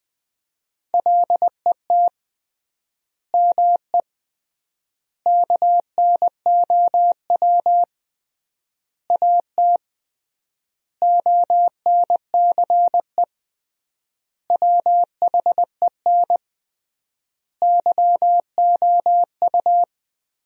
Morse code, heard as a keyed tone, about 21 wpm, 700 Hz. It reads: LET ME KNOW AT ONCE WHEN YOU
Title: 3x Word-Spacing - Sentences from Top 300 Words - 20wpm